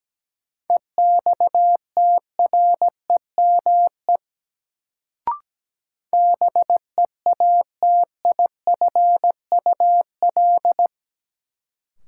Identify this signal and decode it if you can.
Morse code, heard as a keyed tone, about 17 wpm, 700 Hz. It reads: EXTREME BEATIFUL
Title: Sentences from Top 1300 Words - New Word Drill - 17wpm